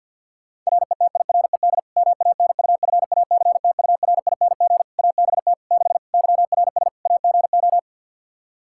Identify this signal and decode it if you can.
Morse code, heard as a keyed tone, about 51 wpm, 700 Hz. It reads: RETIRED MANUFACTURING ABT 6 YRS AGO
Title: Practice QSO - Exchange 3 - More Info - 50wpm